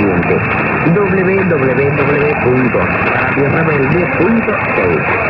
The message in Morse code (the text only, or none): none